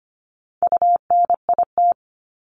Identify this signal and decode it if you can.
Morse code, heard as a keyed tone, about 25 wpm, 700 Hz. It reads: UNIT
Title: Top 500 Words Repeated 25wpm